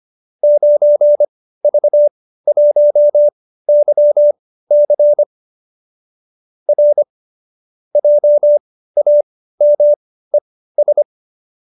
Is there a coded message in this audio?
Yes: Morse code from 0.4 to 11.0 s, heard as a keyed tone, about 25 wpm, spaced as for 15 wpm, 590 Hz. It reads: 9V1YC R JAMES